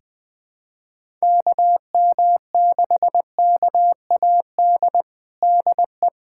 Morse code, heard as a keyed tone, about 21 wpm, 700 Hz. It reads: KM6KAD DE
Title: USA Callsign DE Callsign - 6 Characters - 20wpm